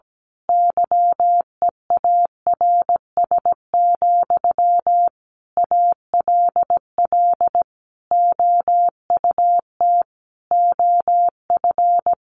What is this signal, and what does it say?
Morse code, heard as a keyed tone, about 17 wpm, 700 Hz. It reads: YEARS, ALL OUT OF
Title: A Sound Of Thunder 17wpm